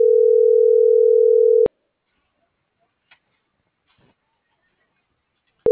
Ringback tone (North America), 440 + 480 Hz.
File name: CENTRAL GENERAL / out america.mejia